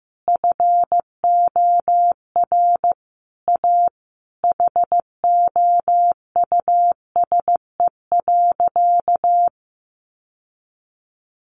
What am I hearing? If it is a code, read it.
Morse code, heard as a keyed tone, about 14 wpm, 700 Hz. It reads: FOR A HOUSE.
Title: Sentences from Top 600 Words 15wpm